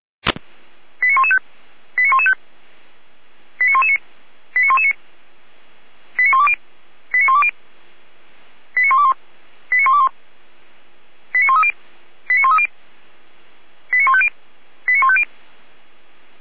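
Five-tone selective call (ZVEI), heard as quick runs of five tones, tones of about 70 ms. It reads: 89115 89115 89119 89119 89120 89120 89121 89121 89130 89130 89150 89150
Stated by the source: ZVEI five-tone selective call received on SDR.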